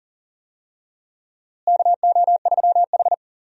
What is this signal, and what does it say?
Morse code, heard as a keyed tone, about 40 wpm, 700 Hz. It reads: KO3H
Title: USA Call Signs - 4 Characters - Repeated 3x - 40wpm